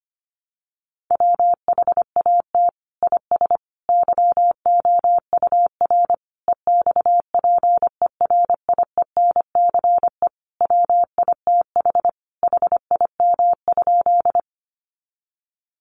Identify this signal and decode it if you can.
Morse code, heard as a keyed tone, about 25 wpm, 700 Hz. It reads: WHAT IS YOUR EXPERIENCE WITH HIM?
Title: Sentences from Top 600 Words 25wpm